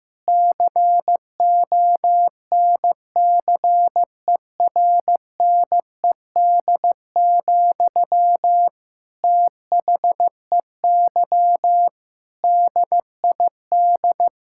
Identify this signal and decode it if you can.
Morse code, heard as a keyed tone, about 15 wpm, 700 Hz. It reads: CONCERNED, THEY DID
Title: Sentences from Top 1000 Words - Review Part 6 - 15wpm